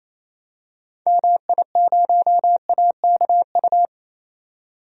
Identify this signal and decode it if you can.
Morse code, heard as a keyed tone, about 27 wpm, 700 Hz. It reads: MI0AKU
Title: DXCC-100 Call Signs - 6 Characters - 28wpm